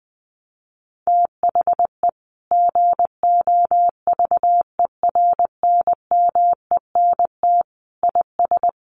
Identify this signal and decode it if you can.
Morse code, heard as a keyed tone, about 20 wpm, 700 Hz. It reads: THE GOVERNMENT IS
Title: Sentences from Top 1300 Words 20wpm